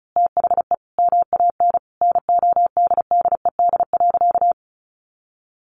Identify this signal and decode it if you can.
Morse code, heard as a keyed tone, about 34 wpm, 700 Hz. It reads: THE MAN NODDED.
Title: A Sound Of Thunder 35wpm